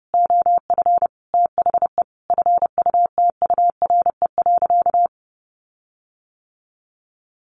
Morse code, heard as a keyed tone, about 30 wpm, 700 Hz. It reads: OF THE FUTURE.
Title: Sentences from Top 1500 Words 30wpm